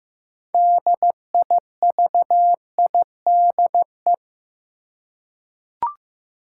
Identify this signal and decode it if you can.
Morse code, heard as a keyed tone, about 15 wpm, 700 Hz. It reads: DIVIDE
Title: Six Letter Words - 15wpm